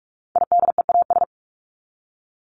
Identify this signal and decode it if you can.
Morse code, heard as a keyed tone, about 44 wpm, 700 Hz. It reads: IDEAS
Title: ES:S Suffix - Top 100 Words with Base Words 45wpm